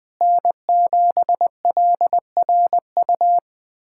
Morse code, heard as a keyed tone, about 20 wpm, 700 Hz. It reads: N7LRU
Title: USA Callsign DE Callsign 20wpm